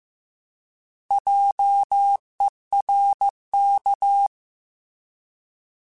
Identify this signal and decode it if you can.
Morse code, heard as a keyed tone, about 15 wpm, 800 Hz. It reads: JERK
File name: Alphabetical / Alphabetical mp3s / Js @13